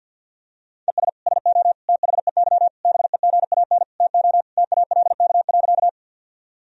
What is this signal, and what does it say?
Morse code, heard as a keyed tone, about 50 wpm, 700 Hz. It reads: ES SO THEY BEGAN TO TALK.
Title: Sentences from Top 400 Words and Abbreviations 50wpm